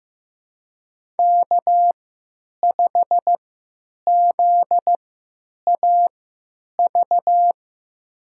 Morse code, heard as a keyed tone, about 15 wpm, spaced as for 9 wpm, 700 Hz. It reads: K5ZAV
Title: USA Call Signs - 5 Characters - 3x Character Spacing - 15wpm